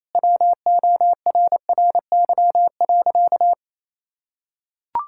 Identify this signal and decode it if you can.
Morse code, heard as a keyed tone, about 28 wpm, 700 Hz. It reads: WORRY.
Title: Pangram Challenge v2 - 28wpm